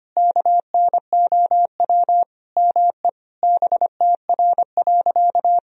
Morse code, heard as a keyed tone, about 25 wpm, 700 Hz. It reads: KNOW ME BTR.
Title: Sentences from Top 400 Words and Abbreviations 25wpm